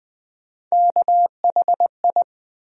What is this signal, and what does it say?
Morse code, heard as a keyed tone, about 20 wpm, 700 Hz. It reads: KHI
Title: USA Call Sign Postfixes - 3 Letters 20wpm